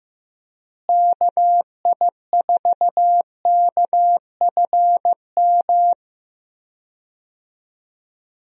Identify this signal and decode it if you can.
Morse code, heard as a keyed tone, about 15 wpm, 700 Hz. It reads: KI4KFM
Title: USA Call Signs - 6 Characters - Repeated 3x - 15wpm